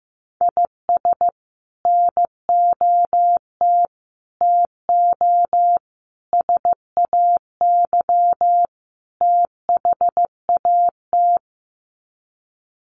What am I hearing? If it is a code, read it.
Morse code, heard as a keyed tone, about 15 wpm, 700 Hz. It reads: IS NOT TO SAY THAT
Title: Sets of 5 Words 15wpm